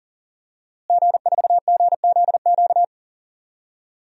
Morse code, heard as a keyed tone, about 40 wpm, 700 Hz. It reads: G4GZQ
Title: DXCC-100 Call Signs - 5 Characters - Repeated 3x - 40wpm